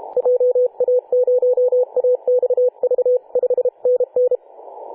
Morse code, heard as a keyed tone, about 32 wpm, 500 Hz. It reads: JA0AXV 5NN
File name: 2013/Sep/21 2035z 3D2GC 18.072MHz CW